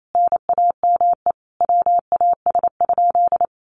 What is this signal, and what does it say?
Morse code, heard as a keyed tone, about 28 wpm, 700 Hz. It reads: NAME WAS?
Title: Sentences from Top 200 Words 28wpm